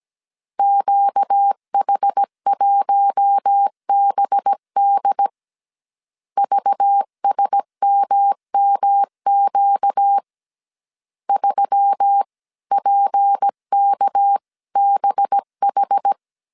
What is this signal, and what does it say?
Morse code, heard as a keyed tone, about 17 wpm, 800 Hz. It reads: QH1BD VSMMQ 3PKBH
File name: 16wpm